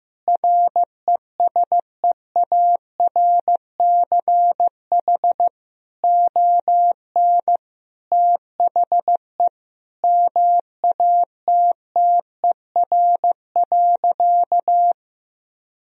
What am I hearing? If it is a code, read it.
Morse code, heard as a keyed tone, about 15 wpm, 700 Hz. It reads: RESEARCH ON THE MATTER.
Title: Sentences from Top 1000 Words - Review Part 4 - 15wpm